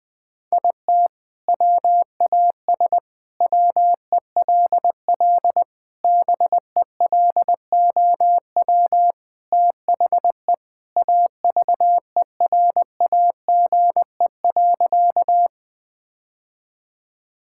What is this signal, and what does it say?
Morse code, heard as a keyed tone, about 21 wpm, 700 Hz. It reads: IT WAS WELL BELOW THE AVERAGE.
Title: Sentences from Top 1000 Words - Review Part 6 - 20wpm